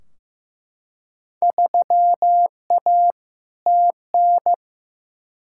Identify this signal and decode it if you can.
Morse code, heard as a keyed tone, about 15 wpm, 700 Hz. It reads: 3A TN
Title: Field Day Exchanges - 15wpm